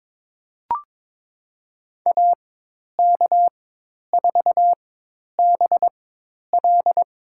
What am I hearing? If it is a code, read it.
Morse code, heard as a keyed tone, about 22 wpm, spaced as for 10 wpm, 700 Hz. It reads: AK4BL
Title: USA Call Signs - 5 Characters - 4x Character Spacing - 22wpm